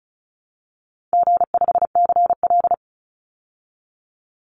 Morse code, heard as a keyed tone, about 34 wpm, 700 Hz. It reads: G5CL
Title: Easy Call Signs Repeated 2x - 35wpm